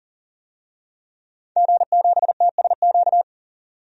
Morse code, heard as a keyed tone, about 40 wpm, 700 Hz. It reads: G7TSQ
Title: DXCC-100 Call Signs - 5 Characters - 40wpm